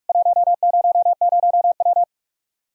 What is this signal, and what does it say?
Morse code, heard as a keyed tone, about 46 wpm, 700 Hz. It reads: 100W
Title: K1USN SST QSO Practice 45wpm